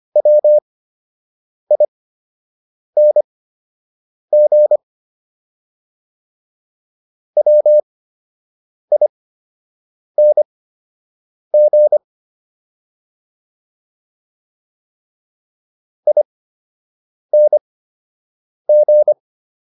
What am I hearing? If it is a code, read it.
Morse code, heard as a keyed tone, about 25 wpm, spaced as for 7 wpm, 600 Hz. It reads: WING WING ING